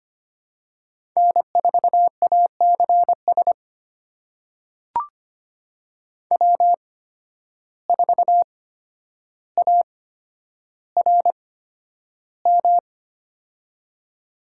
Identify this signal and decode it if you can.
Morse code, heard as a keyed tone, about 25 wpm, spaced as for 11 wpm, 700 Hz. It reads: N4ACS W4ARM
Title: USA Call Signs - 5 Characters - 8x Character Spacing - 25wpm